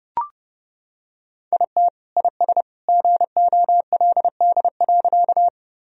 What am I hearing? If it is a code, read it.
Morse code, heard as a keyed tone, about 30 wpm, 700 Hz. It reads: IT IS GOLD.
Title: Sentences from Top 1000 Words - Review Part 3 - 30wpm